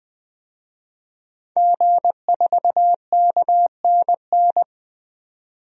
Morse code, heard as a keyed tone, about 20 wpm, 700 Hz. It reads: G4KNN
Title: DXCC-100 Call Signs - 5 Characters - 20wpm